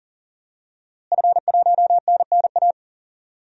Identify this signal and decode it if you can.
Morse code, heard as a keyed tone, about 40 wpm, 700 Hz. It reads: F1NNA
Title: DXCC-100 Call Signs - 5 Characters - Repeated 3x - 40wpm